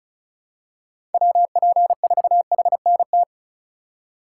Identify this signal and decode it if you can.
Morse code, heard as a keyed tone, about 36 wpm, 700 Hz. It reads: WP4HNT
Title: USA Call Signs - 6 Characters - 4x Character Spacing - 35wpm